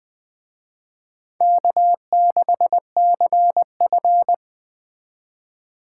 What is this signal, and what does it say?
Morse code, heard as a keyed tone, about 20 wpm, 700 Hz. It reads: K6CF